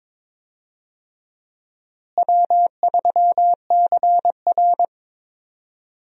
Morse code, heard as a keyed tone, about 22 wpm, 700 Hz. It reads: W3CR
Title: USA Call Signs - 4 Characters - Repeated 2x - 22wpm